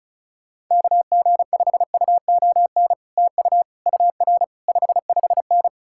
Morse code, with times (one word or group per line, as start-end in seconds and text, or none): 0.7-2.9 KG5UON
3.2-3.6 TU
3.9-4.4 UR
4.7-5.7 55N